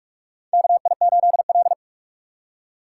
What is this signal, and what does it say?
Morse code, heard as a keyed tone, about 42 wpm, 700 Hz. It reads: KI8L
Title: USA Call Signs - 4 Characters - 2x Character Spacing - 45wpm